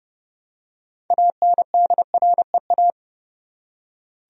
Morse code, heard as a keyed tone, about 30 wpm, 700 Hz. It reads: ANDREA
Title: USA Common Names - 30wpm